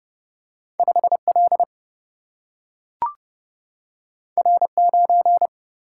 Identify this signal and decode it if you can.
Morse code, heard as a keyed tone, about 30 wpm, 700 Hz. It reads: 5 L R 9
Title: Single-Letter Single-Number 30wpm